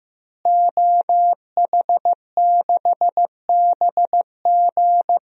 Morse code, heard as a keyed tone, about 15 wpm, 700 Hz. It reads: OH6BG
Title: DXCC-100 Call Signs - 5 Characters - Repeated 2x - 15wpm